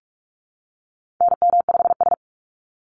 Morse code, heard as a keyed tone, about 44 wpm, 700 Hz. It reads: NM5S
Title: NA SOTA Activator Callsigns 45wpm